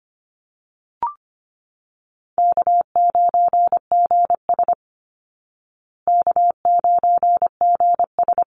Morse code, heard as a keyed tone, about 25 wpm, 700 Hz. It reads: K9GS K9GS
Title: Call Signs Repeated 3x - 25wpm